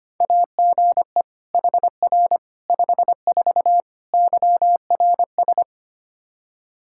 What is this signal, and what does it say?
Morse code, heard as a keyed tone, about 25 wpm, 700 Hz. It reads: AGE HR 54 YRS
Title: Practice QSO - Exchange 3 - More Info - 25wpm